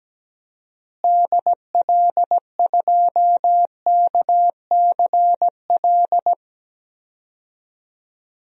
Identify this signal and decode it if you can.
Morse code, heard as a keyed tone, about 17 wpm, 700 Hz. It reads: DL2KCL
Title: DXCC-100 Call Signs - 6 Characters - Repeated 3x - 17wpm